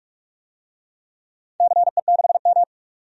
Morse code, heard as a keyed tone, about 44 wpm, 700 Hz. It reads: CE6M
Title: DXCC-100 Call Signs - 4 Characters - 45wpm